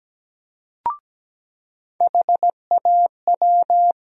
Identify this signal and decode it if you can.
Morse code, heard as a keyed tone, about 17 wpm, 700 Hz. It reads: HAW
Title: USA Call Sign Postfixes - 3 Letters 17wpm